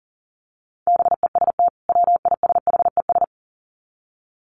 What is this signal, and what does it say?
Morse code, heard as a keyed tone, about 40 wpm, 700 Hz. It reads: BEST WISHES
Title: Common QSO Phrases 40wpm